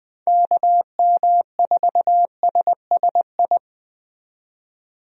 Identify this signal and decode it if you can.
Morse code, heard as a keyed tone, about 21 wpm, 700 Hz. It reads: KM4SSI
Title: USA Call Signs - 6 Characters - 20wpm